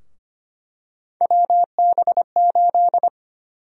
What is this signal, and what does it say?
Morse code, heard as a keyed tone, about 25 wpm, 700 Hz. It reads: WB8
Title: USA Call Sign Prefixes - Repeated 2x - 25wpm